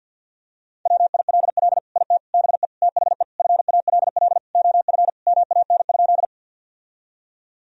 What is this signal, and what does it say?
Morse code, heard as a keyed tone, about 50 wpm, 700 Hz. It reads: WILL IT BE THE FALL OF MAN?
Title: Sentences from Top 1000 Words - Review Part 2 - 50wpm